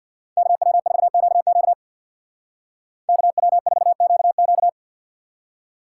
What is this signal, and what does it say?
Morse code, heard as a keyed tone, about 55 wpm, 700 Hz. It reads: KW4QQ KW4QQ
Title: Speed Racing - USA Call Signs 50wpm